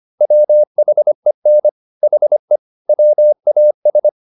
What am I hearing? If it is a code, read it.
Morse code, heard as a keyed tone, about 25 wpm, 600 Hz. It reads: WHEN HE WAS